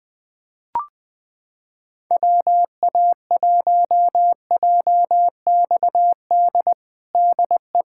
Morse code, heard as a keyed tone, about 21 wpm, 700 Hz. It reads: WA1JXD DE
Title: USA Callsign DE Callsign - 6 Characters - 20wpm